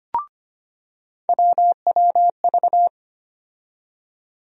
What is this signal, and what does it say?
Morse code, heard as a keyed tone, about 25 wpm, 700 Hz. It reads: WWV
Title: USA Call Sign Postfixes - 3 Letters 25wpm